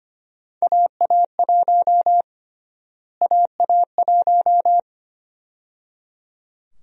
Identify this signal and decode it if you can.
Morse code, heard as a keyed tone, about 25 wpm, 700 Hz. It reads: AA1 AA1
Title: USA Call Sign Prefixes - Repeated 2x - 25wpm